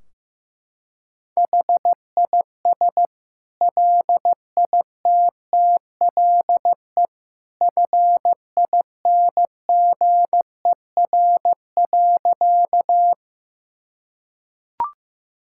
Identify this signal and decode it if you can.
Morse code, heard as a keyed tone, about 16 wpm, 700 Hz. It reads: HIS LITTLE FINGER.
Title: Sentences from Top 1600 Words 15wpm